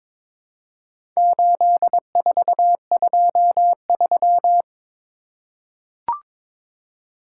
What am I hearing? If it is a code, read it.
Morse code, heard as a keyed tone, about 21 wpm, 700 Hz. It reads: 8423
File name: practice / 22wpm / Four Letter-Number Combinations - 22wpm